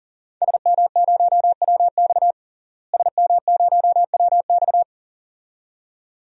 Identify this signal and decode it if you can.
Morse code, heard as a keyed tone, about 37 wpm, 700 Hz. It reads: SM0WX SM0WX
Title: DXCC-100 Call Signs - 5 Characters - Repeated 2x - 40wpm